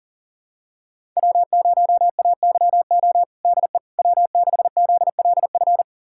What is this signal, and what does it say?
Morse code, heard as a keyed tone, about 40 wpm, 700 Hz. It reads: W0AYO DE W6ZLF
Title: USA Callsign DE Callsign - 5 Characters - 40wpm